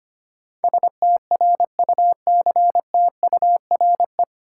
Morse code, heard as a keyed tone, about 25 wpm, 700 Hz. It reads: STRUCTURE
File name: Nine Letter Words - 25wpm